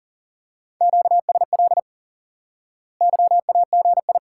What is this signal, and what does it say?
Morse code, heard as a keyed tone, about 40 wpm, 700 Hz. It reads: QSL YAGI
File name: IQE Course 3 - Lesson 89 - YAGI - Two Elements - 40wpm